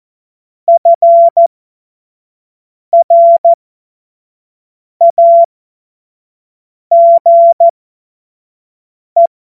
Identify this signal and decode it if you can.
Morse code, heard as a keyed tone, about 14 wpm, spaced as for 5 wpm, 680 Hz. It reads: FRAGE
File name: Morsekurs / Witze / Witz 16. Februar 2021, 25 F70